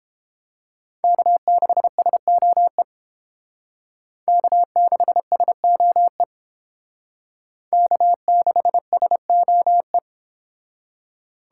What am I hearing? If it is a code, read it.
Morse code, heard as a keyed tone, about 30 wpm, 700 Hz. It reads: K6SOE K6SOE K6SOE